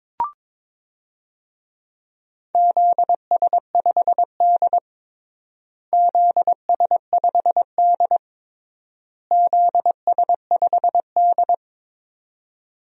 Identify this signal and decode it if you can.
Morse code, heard as a keyed tone, about 22 wpm, 700 Hz. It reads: ZS5D ZS5D ZS5D
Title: DXCC-100 Call Signs - 4 Characters - Repeated 3x - 22wpm